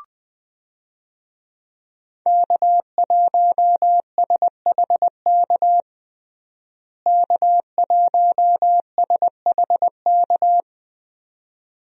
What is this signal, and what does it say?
Morse code, heard as a keyed tone, about 20 wpm, 700 Hz. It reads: K1SHK K1SHK
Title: USA Call Signs - 5 Characters - Repeated 2x - 20wpm